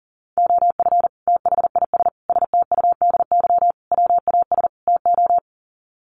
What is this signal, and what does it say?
Morse code, heard as a keyed tone, about 40 wpm, 700 Hz. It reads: OF THIS STUDY WAS TO
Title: Sets of 5 Words 40wpm